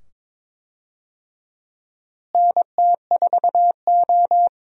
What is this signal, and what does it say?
Morse code, heard as a keyed tone, about 22 wpm, 700 Hz. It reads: NT4O
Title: USA Call Signs - 4 Characters - Repeated 2x - 22wpm